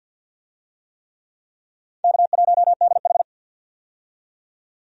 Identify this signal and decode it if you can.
Morse code, heard as a keyed tone, about 52 wpm, 700 Hz. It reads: K1DH
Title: USA Call Signs - 4 Characters - 50wpm